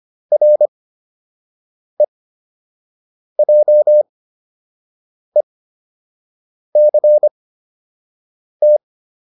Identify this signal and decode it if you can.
Morse code, heard as a keyed tone, about 25 wpm, spaced as for 6 wpm, 600 Hz. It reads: REJECT